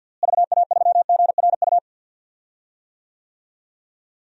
Morse code, heard as a keyed tone, about 47 wpm, 700 Hz. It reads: VA3GRU
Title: DXCC-100 Call Signs - 6 Characters - Repeated 2x - 50wpm